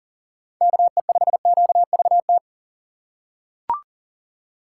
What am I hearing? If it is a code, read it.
Morse code, heard as a keyed tone, about 38 wpm, 700 Hz. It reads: KE5QVT
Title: USA Call Signs - 4x Character Spacing - 40wpm